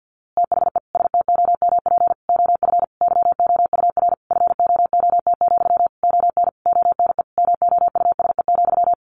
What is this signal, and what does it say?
Morse code, heard as a keyed tone, about 50 wpm, 700 Hz. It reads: THE STOMP OF YOUR FOOT, ON ONE MOUSE,
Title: A Sound Of Thunder 50wpm